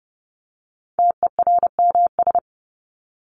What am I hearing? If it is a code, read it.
Morse code, heard as a keyed tone, about 30 wpm, 700 Hz. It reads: TERMS
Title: ES:S Suffix - Top 100 Words with Base Words 30wpm